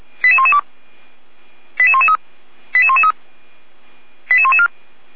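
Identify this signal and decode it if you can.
Five-tone selective call (ZVEI), heard as quick runs of five tones, tones of about 70 ms. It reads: 88112 88113 88113 88114
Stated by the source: ZVEI five-tone call 88111 received on SDR.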